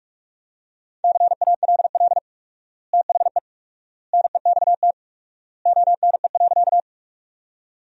Morse code, heard as a keyed tone, about 45 wpm, 700 Hz. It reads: CALL THE NEXT ONE.
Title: 4x Word-Spacing - Sentences from Top 300 Words - 45wpm